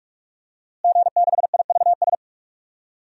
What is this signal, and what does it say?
Morse code, heard as a keyed tone, about 47 wpm, 700 Hz. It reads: G6IVS
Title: DXCC-100 Call Signs - 5 Characters - Repeated 3x - 45wpm